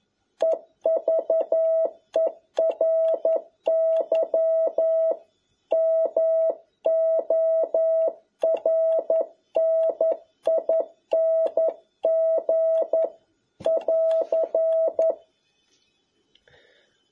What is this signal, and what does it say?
Morse code, heard as a keyed tone, about 11 wpm, 650 Hz. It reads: EVERY MORNING <AR>
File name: jh8cbh / 2021cw